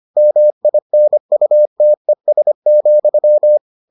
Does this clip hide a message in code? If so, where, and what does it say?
Yes: Morse code from 0.2 to 3.6 s, heard as a keyed tone, about 25 wpm, 600 Hz. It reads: MINUTES,